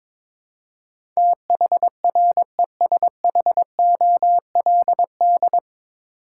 Morse code, heard as a keyed tone, about 22 wpm, 700 Hz. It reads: THRESHOLD
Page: Morse Code Ninja practice 22wpm